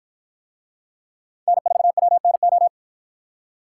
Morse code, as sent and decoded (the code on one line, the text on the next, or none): -. ....- .-- -. ---
N4WNO